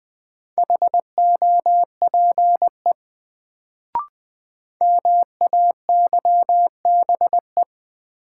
Morse code, heard as a keyed tone, about 20 wpm, 700 Hz. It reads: HOPE MAYBE
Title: Sentences from Top 1000 Words - New Word Drill - 20wpm